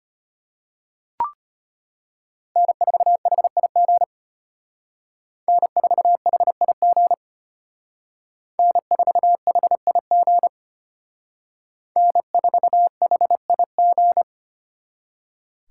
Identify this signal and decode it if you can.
Morse code, heard as a keyed tone, about 33 wpm, 700 Hz. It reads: N4HIG N4HIG N4HIG N4HIG